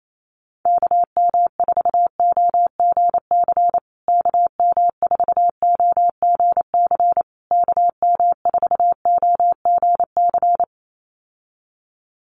Morse code, heard as a keyed tone, about 28 wpm, 700 Hz. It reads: KM4OGC KM4OGC KM4OGC
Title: USA Call Signs Repeated 3x 28wpm